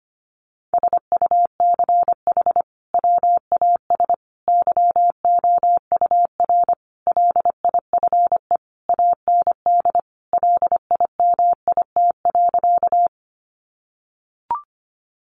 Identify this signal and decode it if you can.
Morse code, heard as a keyed tone, about 25 wpm, 700 Hz. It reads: SUCH WAS YOUR LIFE AND LIMIT.
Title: Sentences from Top 1500 Words 25wpm